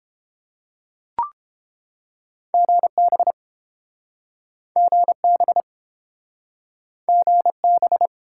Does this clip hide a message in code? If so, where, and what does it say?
Yes: Morse code from 2.5 to 8.1 s, heard as a keyed tone, about 30 wpm, 700 Hz. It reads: GB GB GB